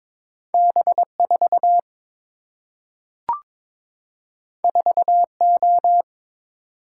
Morse code, heard as a keyed tone, about 22 wpm, 700 Hz. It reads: B 4 4 O
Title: Single-Letter Single-Number 22wpm